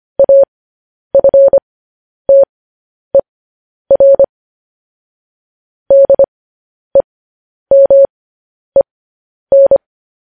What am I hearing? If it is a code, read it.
Morse code, heard as a keyed tone, about 25 wpm, spaced as for 10 wpm, 550 Hz. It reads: AFTER DEMEN